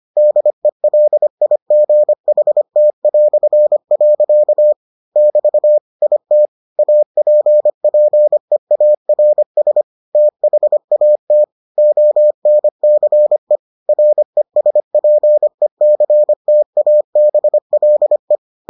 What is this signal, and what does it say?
Morse code, heard as a keyed tone, about 25 wpm, 600 Hz. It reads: DELIGHT". <BT> IT APPEARS THAT ONCE RESPECTABLE